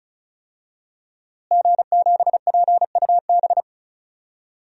Morse code, heard as a keyed tone, about 34 wpm, 700 Hz. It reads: G7PUB
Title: DXCC-100 Call Signs - 5 Characters - Repeated 3x - 35wpm